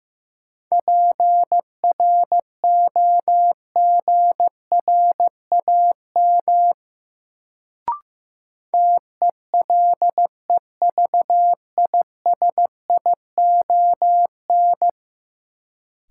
Morse code, heard as a keyed tone, about 16 wpm, 700 Hz. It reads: PROGRAM TELEVISION
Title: Sentences from Top 1000 Words - Review Part 4 - New Word Drill - 15wpm